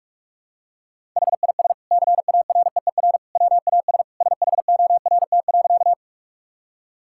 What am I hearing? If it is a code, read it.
Morse code, heard as a keyed tone, about 46 wpm, 700 Hz. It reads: HIS CAREER WAS SHORT.
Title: Sentences from Top 1000 Words - Review Part 6 - 45wpm